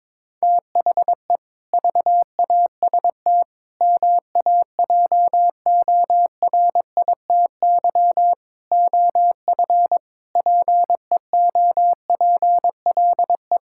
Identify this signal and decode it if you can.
Morse code, heard as a keyed tone, about 22 wpm, 700 Hz. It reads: THE VAST MAJORITY OF PEOPLE
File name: practice / 22wpm / Sentences with 4-Word Phrases - 22wpm